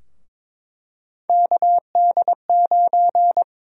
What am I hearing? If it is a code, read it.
Morse code, heard as a keyed tone, about 22 wpm, 700 Hz. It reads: KD9
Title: USA Call Sign Prefixes - Repeated 2x - 22wpm